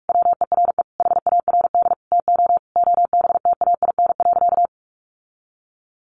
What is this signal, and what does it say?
Morse code, heard as a keyed tone, about 44 wpm, 700 Hz. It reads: WERE HARD TO OBTAIN.
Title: Sentences from Top 1300 Words 45wpm